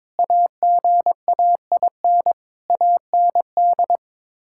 Morse code, heard as a keyed tone, about 22 wpm, 700 Hz. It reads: AGAIN AND
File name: Sets of 9 Words - 22wpm